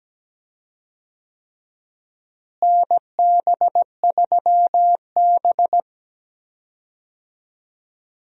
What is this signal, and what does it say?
Morse code, heard as a keyed tone, about 17 wpm, 700 Hz. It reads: NB3B
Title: USA Call Signs - 4 Characters - Repeated 3x - 17wpm